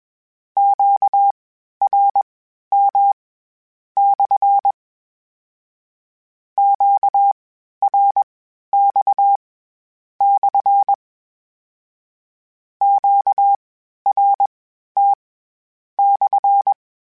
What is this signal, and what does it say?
Morse code, heard as a keyed tone, about 21 wpm, spaced as for 11 wpm, 800 Hz. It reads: QRM/ QRX/ QRT/
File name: Q Calls / Q Calls-mp3s / Q Calls SPSL1 @20